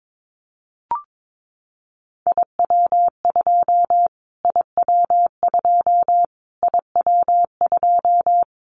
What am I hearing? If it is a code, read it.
Morse code, heard as a keyed tone, about 22 wpm, 700 Hz. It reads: IW2 IW2 IW2
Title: Callsign Prefixes for Top 25 DXCC - Repeated 3x - 22wpm